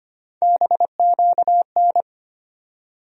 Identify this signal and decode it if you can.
Morse code, heard as a keyed tone, about 25 wpm, 700 Hz. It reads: BQN
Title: USA Call Sign Postfixes - 3 Letters 25wpm